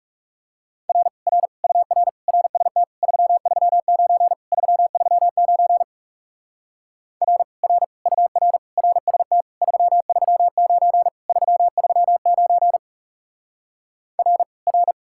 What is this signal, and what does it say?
Morse code, heard as a keyed tone, about 36 wpm, 700 Hz. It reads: R R UR RST 339 339 R R UR RST 339 339 R R
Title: Speed-Racing Common QSO Phrases - 30wpm